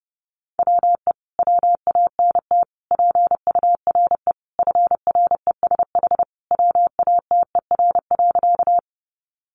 Morse code, heard as a keyed tone, about 30 wpm, 700 Hz. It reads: WE WANT PURE FRESH WATER.
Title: Sentences from Top 1500 Words 30wpm